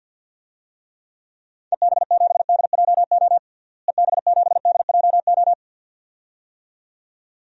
Morse code, heard as a keyed tone, about 49 wpm, 700 Hz. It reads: EB7DJO EB7DJO
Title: DXCC-100 Call Signs - 6 Characters - Repeated 2x - 50wpm